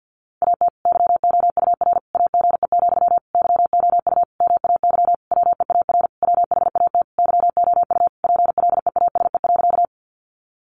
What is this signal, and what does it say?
Morse code, heard as a keyed tone, about 49 wpm, 700 Hz. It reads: AT YOUR AGE, YOU MAY WEAR WHAT YOU PLEASE.
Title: Sentences from Top 1500 Words 50wpm